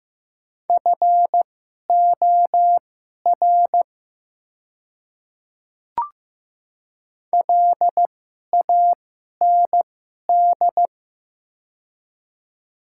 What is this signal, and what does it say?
Morse code, heard as a keyed tone, about 15 wpm, spaced as for 11 wpm, 700 Hz. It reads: FOR LAND
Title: 2x Character Spacing - Top 200 Words - 15wpm